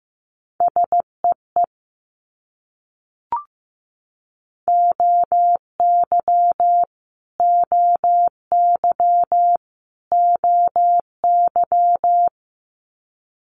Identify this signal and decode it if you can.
Morse code, heard as a keyed tone, about 15 wpm, 700 Hz. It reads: SEE OY OY OY
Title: Multi-letter Phonemes Repeated 3x 15wpm